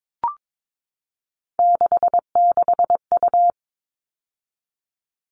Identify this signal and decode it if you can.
Morse code, heard as a keyed tone, about 22 wpm, 700 Hz. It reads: 66U
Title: All Cut Numbers - 3-Digit 22wpm